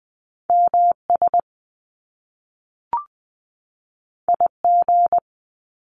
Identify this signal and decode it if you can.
Morse code, heard as a keyed tone, about 20 wpm, 700 Hz. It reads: MS IG
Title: Top Two Letter Combinations in Words 20wpm